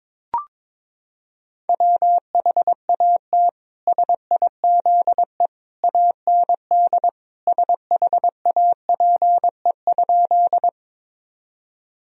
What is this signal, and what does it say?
Morse code, heard as a keyed tone, about 22 wpm, 700 Hz. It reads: WHAT SIZE AND SHAPE?
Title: Sentences from Top 1000 Words - Review Part 3 - 22wpm